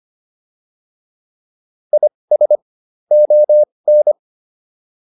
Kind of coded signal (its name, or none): Morse code